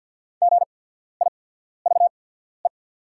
Morse code, heard as a keyed tone, about 48 wpm, spaced as for 14 wpm, 700 Hz. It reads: GIVE
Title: 8x Character Spacing - Top 300 Words - 50wpm